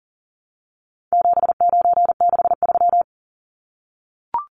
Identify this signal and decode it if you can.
Morse code, heard as a keyed tone, about 40 wpm, 700 Hz. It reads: 7963
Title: Numbers - 4-Digit 40wpm